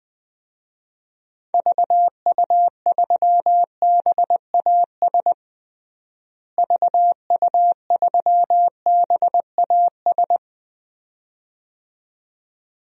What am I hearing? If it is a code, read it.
Morse code, heard as a keyed tone, about 20 wpm, 700 Hz. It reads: VU3BAS VU3BAS
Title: DXCC-100 Call Signs - 6 Characters - Repeated 2x - 20wpm